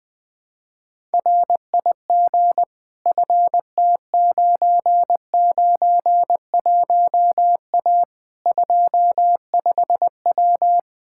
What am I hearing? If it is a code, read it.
Morse code, heard as a keyed tone, about 20 wpm, 700 Hz. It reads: RIG FT991A 25W
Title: Practice QSO - Exchange 2 - Info - 20wpm